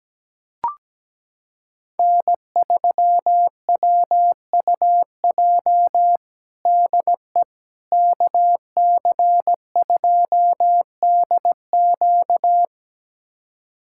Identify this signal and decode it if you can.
Morse code, heard as a keyed tone, about 17 wpm, 700 Hz. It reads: N3WUJ DE KC2DQ
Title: USA Callsign DE Callsign - 5 Characters - 17wpm